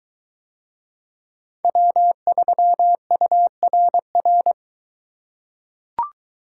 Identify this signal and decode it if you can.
Morse code, heard as a keyed tone, about 21 wpm, 700 Hz. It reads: W3URR